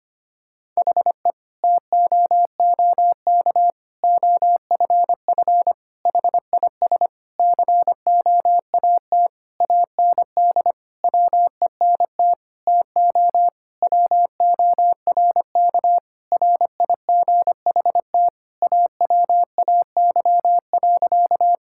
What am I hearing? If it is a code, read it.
Morse code, heard as a keyed tone, about 25 wpm, 700 Hz. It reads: HE TOOK OFF HIS COAT AND WENT TO WORK RIGHT AWAY.
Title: Sentences from Top 1600 Words 25wpm